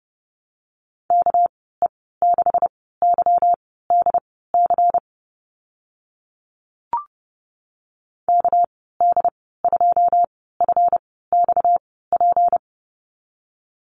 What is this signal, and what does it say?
Morse code, heard as a keyed tone, about 30 wpm, spaced as for 17 wpm, 700 Hz. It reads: KE6YDC KD2FXP
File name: Lesson 37g - Letter x - Call Signs 30wpm